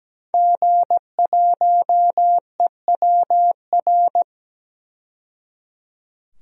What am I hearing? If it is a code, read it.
Morse code, heard as a keyed tone, about 17 wpm, 700 Hz. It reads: G1EWR
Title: DXCC-100 Call Signs - 5 Characters - 17wpm